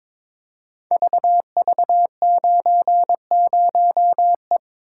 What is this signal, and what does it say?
Morse code, heard as a keyed tone, about 22 wpm, 700 Hz. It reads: VV90E
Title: Code Groups - 5 Letter:Numbers - 22wpm